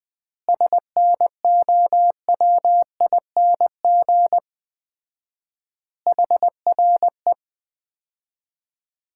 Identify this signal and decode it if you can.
Morse code, heard as a keyed tone, about 20 wpm, 700 Hz. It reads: SNOWING HRE
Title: Common QSO Phrases - 4x Word-Spacing - 20wpm